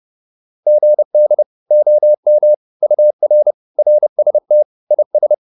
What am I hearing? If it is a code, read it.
Morse code, heard as a keyed tone, about 30 wpm, 600 Hz. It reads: GD OM UR RST IS
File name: QSO406 30